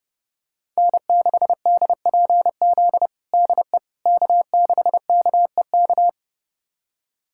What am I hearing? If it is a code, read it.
Morse code, heard as a keyed tone, about 30 wpm, 700 Hz. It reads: N6DPZ DE K6KEK
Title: USA Callsign DE Callsign - 5 Characters - 30wpm